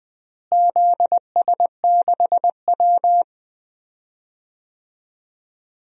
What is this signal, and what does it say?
Morse code, heard as a keyed tone, about 20 wpm, 700 Hz. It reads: ZS6W
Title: DXCC-100 Call Signs - 4 Characters - Repeated 3x - 20wpm